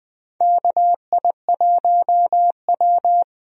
Morse code, heard as a keyed tone, about 19 wpm, 700 Hz. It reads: KI1W
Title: USA Call Signs - 4 Characters - Repeated 3x - 20wpm